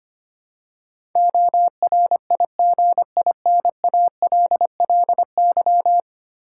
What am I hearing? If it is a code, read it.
Morse code, heard as a keyed tone, about 25 wpm, 690 Hz. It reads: ORIGINALLY
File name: 10 Letter Words - First 3 Letters Given - 25wpm